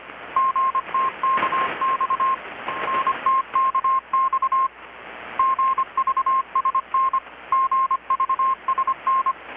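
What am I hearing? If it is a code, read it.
Morse code, heard as a keyed tone, about 25 wpm, 1060 Hz. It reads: GTKX NTKX GVSN GVSN
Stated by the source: Morse from KPH's Night of Nights XX maritime radiotelegraphy event, received in Chiapas, Mexico about 3,800 km away on a Kenwood R-600 with a grounded 10 m long-wire antenna.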